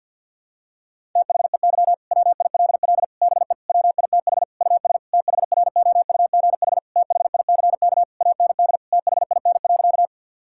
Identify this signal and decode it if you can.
Morse code, heard as a keyed tone, about 50 wpm, 700 Hz. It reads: THEY WILL BE WITH US THROUGH THICK AND THIN.
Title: Sentences from Top 1600 Words 50wpm